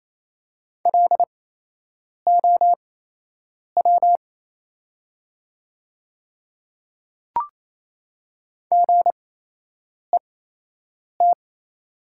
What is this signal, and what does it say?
Morse code, heard as a keyed tone, about 27 wpm, spaced as for 8 wpm, 700 Hz. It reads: LOW GET
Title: 8x Character Spacing - Top 200 Words - 28wpm